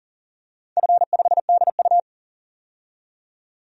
Morse code, heard as a keyed tone, about 40 wpm, 700 Hz. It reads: F5DU
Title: DXCC-100 Call Signs - 4 Characters - 40wpm